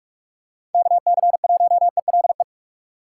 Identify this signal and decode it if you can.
Morse code, heard as a keyed tone, about 41 wpm, 700 Hz. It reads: KC1ELE
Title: USA Call Signs - 6 Characters - 4x Character Spacing - 45wpm